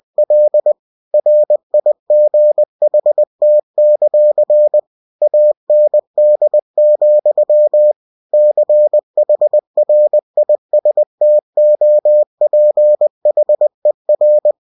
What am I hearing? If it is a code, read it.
Morse code, heard as a keyed tone, about 20 wpm, 600 Hz. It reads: L RIGHT; AND, CHRISTOPHER
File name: winnie-20wpm